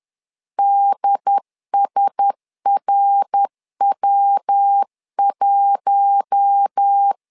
Morse code, heard as a keyed tone, about 10 wpm, 800 Hz. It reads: DSRW1
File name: LEZIONI / 4 Q1S 10wpm 2133